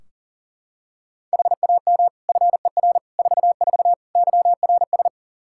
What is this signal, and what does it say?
Morse code, heard as a keyed tone, about 40 wpm, 700 Hz. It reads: HAM FER 44 YRS
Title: K1USN SST QSO Practice 40wpm